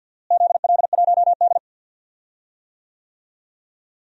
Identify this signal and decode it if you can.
Morse code, heard as a keyed tone, about 50 wpm, 700 Hz. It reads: ZL1D
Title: DXCC-100 Call Signs - 4 Characters - Repeated 3x - 50wpm